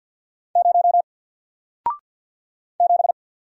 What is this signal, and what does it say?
Morse code, heard as a keyed tone, about 50 wpm, 700 Hz. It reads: 0 7
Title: Single Letter-Number Alt 50wpm